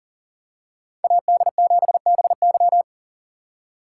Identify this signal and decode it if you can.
Morse code, heard as a keyed tone, about 40 wpm, 700 Hz. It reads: AD7BY
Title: USA Call Signs - 4x Character Spacing - 40wpm